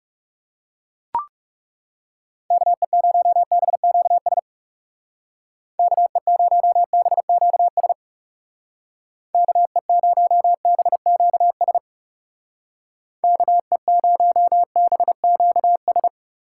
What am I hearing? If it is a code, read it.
Morse code, heard as a keyed tone, about 36 wpm, 700 Hz. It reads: KE0BQS KE0BQS KE0BQS KE0BQS
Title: Speed Racing - USA 2x3 Call Signs 30wpm